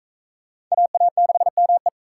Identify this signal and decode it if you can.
Morse code, heard as a keyed tone, about 39 wpm, 700 Hz. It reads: AA6ME